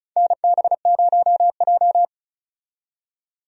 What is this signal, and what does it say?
Morse code, heard as a keyed tone, about 34 wpm, 700 Hz. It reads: NB0J
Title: USA Call Signs - 4 Characters - 35wpm